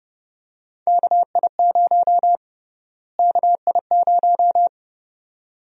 Morse code, heard as a keyed tone, about 30 wpm, 700 Hz. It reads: KI0 KI0
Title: USA Call Sign Prefixes - Repeated 2x - 30wpm